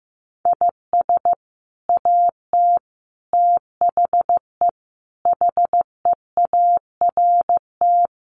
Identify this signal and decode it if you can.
Morse code, heard as a keyed tone, about 15 wpm, 700 Hz. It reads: IS AT THE HEART
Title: Sets of 5 Words Encore2 15wpm